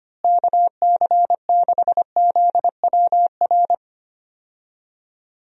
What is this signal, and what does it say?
Morse code, heard as a keyed tone, about 25 wpm, 700 Hz. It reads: KC6ZWR
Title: USA Call Signs - 6 Characters - Repeated 3x - 25wpm